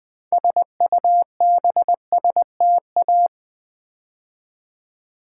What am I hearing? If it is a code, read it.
Morse code, heard as a keyed tone, about 20 wpm, 700 Hz. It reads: SUBSTA
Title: Build A Word - 3 Letters At A Time - 9-Letter Words - 20wpm